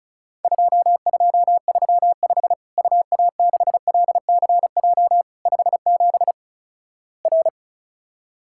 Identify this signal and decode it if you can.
Morse code, heard as a keyed tone, about 36 wpm, 700 Hz. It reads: 2235 UA6LCJ 57 R
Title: Worked All Europe DX Contest - QTC Exchange - 35wpm